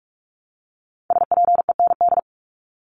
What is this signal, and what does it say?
Morse code, heard as a keyed tone, about 44 wpm, 700 Hz. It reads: SPEND
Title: Top 501-750 Words 45wpm